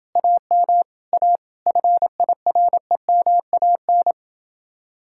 Morse code, heard as a keyed tone, about 28 wpm, 700 Hz. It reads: AM A FIREMAN